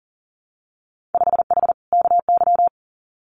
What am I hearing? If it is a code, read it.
Morse code, heard as a keyed tone, about 40 wpm, 700 Hz. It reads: 5H KY
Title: Winter Field Day Exchanges Encore 40wpm